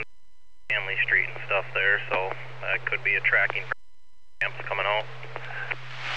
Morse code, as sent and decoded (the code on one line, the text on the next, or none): none